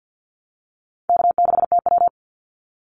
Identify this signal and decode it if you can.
Morse code, heard as a keyed tone, about 50 wpm, 700 Hz. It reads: K6TW
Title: NA SOTA Activator Callsigns 50wpm